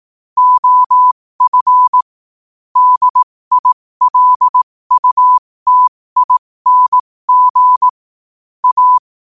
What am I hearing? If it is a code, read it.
Morse code, heard as a keyed tone, about 18 wpm, spaced as for 15 wpm, 1000 Hz. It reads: OF DILUTING A
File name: qotdfiles / 15wpm